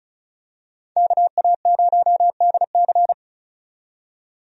Morse code, heard as a keyed tone, about 36 wpm, 700 Hz. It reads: KA0DC
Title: USA Call Signs - 5 Characters - 35wpm